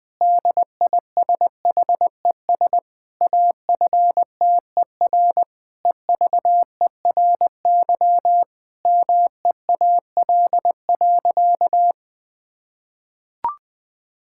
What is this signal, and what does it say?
Morse code, heard as a keyed tone, about 20 wpm, 700 Hz. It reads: DISHES AFTER EVERY MEAL.
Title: Difficult Sentences with Hints - 20wpm